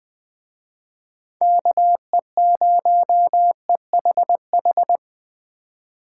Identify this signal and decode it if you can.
Morse code, heard as a keyed tone, about 21 wpm, 700 Hz. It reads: KE0EHH
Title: Speed Racing - USA 2x3 Call Signs 17wpm